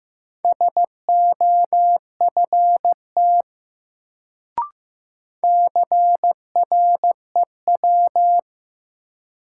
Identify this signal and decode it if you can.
Morse code, heard as a keyed tone, about 15 wpm, 700 Hz. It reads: SOFT CREW
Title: Sentences from Top 1500 Words - New Word Drill - 15wpm